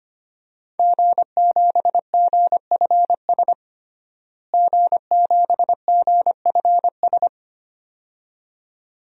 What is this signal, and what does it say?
Morse code, heard as a keyed tone, about 25 wpm, 700 Hz. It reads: G7GFS G7GFS
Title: DXCC-100 Call Signs - 5 Characters - Repeated 2x - 25wpm